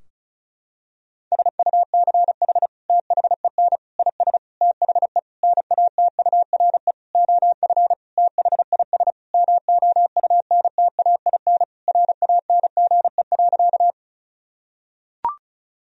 Morse code, heard as a keyed tone, about 36 wpm, 700 Hz. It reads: SUCH THEN IS THE NATURE OF THIS MOUNTAIN RANGE.
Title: Sentences from Top 1000 Words - Review Part 4 - 35wpm